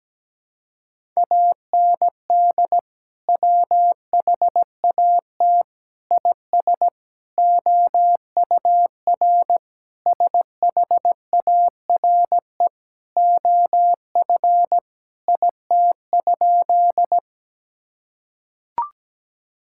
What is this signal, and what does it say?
Morse code, heard as a keyed tone, about 17 wpm, 700 Hz. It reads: AND WHAT IS OUR SHARE OF IT?
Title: Sentences from Top 1000 Words - Review Part 6 - 17wpm